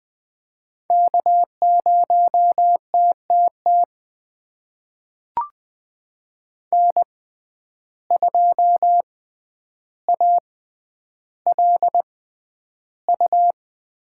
Morse code, heard as a keyed tone, about 21 wpm, spaced as for 10 wpm, 700 Hz. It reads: K0TTT N2ALU
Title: USA Call Signs - 5 Characters - 6x Character Spacing - 20wpm